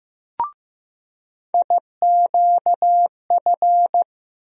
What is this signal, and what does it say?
Morse code, heard as a keyed tone, about 15 wpm, 700 Hz. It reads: IQF
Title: USA Call Sign Postfixes - 3 Letters 15wpm